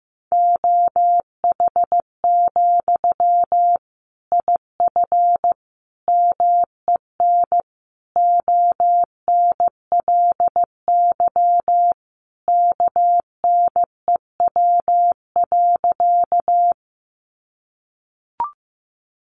Morse code, heard as a keyed tone, about 15 wpm, 700 Hz. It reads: OH, IF MEN ONLY KNEW.
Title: Sentences from Top 500 Words 15wpm